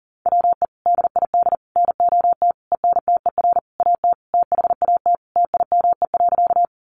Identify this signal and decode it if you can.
Morse code, heard as a keyed tone, about 40 wpm, 700 Hz. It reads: WE DID NOT ENTER AT THAT TIME.
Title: Sentences from Top 1300 Words 40wpm